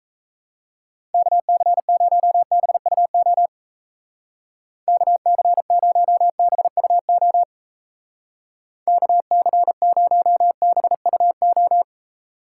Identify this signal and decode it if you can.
Morse code, heard as a keyed tone, about 34 wpm, 700 Hz. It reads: KC0BUO KC0BUO KC0BUO
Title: Speed Racing - USA 2x3 Call Signs 28wpm